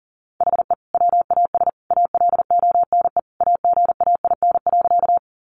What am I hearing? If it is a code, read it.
Morse code, heard as a keyed tone, about 40 wpm, 700 Hz. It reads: HE WAS ALONE AGAIN.
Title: Sentences from Top 1000 Words 40wpm